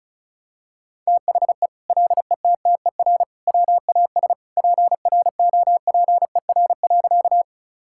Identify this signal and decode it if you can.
Morse code, heard as a keyed tone, about 36 wpm, 700 Hz. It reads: THE LETTER WAS PROPER.
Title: Sentences from Top 1000 Words - Review Part 5 - 35wpm